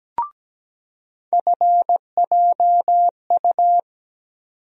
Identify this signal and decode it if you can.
Morse code, heard as a keyed tone, about 17 wpm, 700 Hz. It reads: FJU
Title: USA Call Sign Postfixes - 3 Letters 17wpm